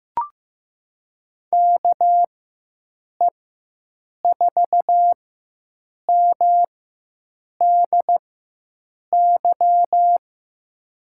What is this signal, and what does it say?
Morse code, heard as a keyed tone, about 15 wpm, spaced as for 7 wpm, 700 Hz. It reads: KE4MDY
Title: USA Call Signs - 6 Characters - 4x Character Spacing - 15wpm